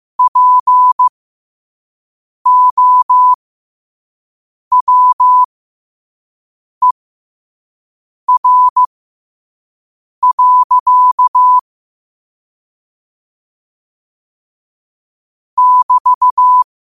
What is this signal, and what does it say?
Morse code, heard as a keyed tone, about 15 wpm, spaced as for 5 wpm, 1000 Hz. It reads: POWER. <BT>
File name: qotdfiles / 5wpm